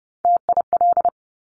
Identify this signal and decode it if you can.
Morse code, heard as a keyed tone, about 30 wpm, 690 Hz. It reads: TIL
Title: Top Three Letter Combinations in Words 30wpm